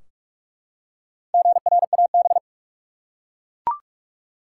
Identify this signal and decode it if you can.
Morse code, heard as a keyed tone, about 42 wpm, 700 Hz. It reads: GRAB
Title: Four Letter Words - 45wpm